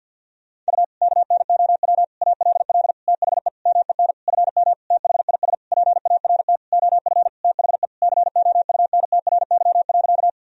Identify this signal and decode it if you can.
Morse code, heard as a keyed tone, about 52 wpm, 700 Hz. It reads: U KNOW ALL THE MEN FM THIS PART OF THE COUNTRY.
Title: Sentences from Top 300 Words and Abbreviations 50wpm